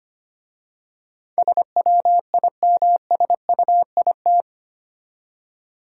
Morse code, heard as a keyed tone, about 25 wpm, 690 Hz. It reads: SWIMSUIT
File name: practice / 25wpm / Easy Words with Hints - 25wpm